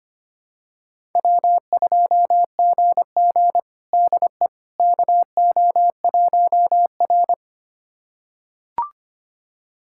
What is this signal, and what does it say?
Morse code, heard as a keyed tone, about 25 wpm, 700 Hz. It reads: W2GG DE KO1R
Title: USA Callsign DE Callsign - 4 Characters - 25wpm